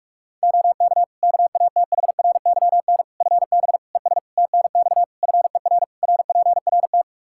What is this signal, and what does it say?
Morse code, heard as a keyed tone, about 41 wpm, 700 Hz. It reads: OK KATHRYN FB ES TNX FER RPRT
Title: K1USN SST QSO Practice 45wpm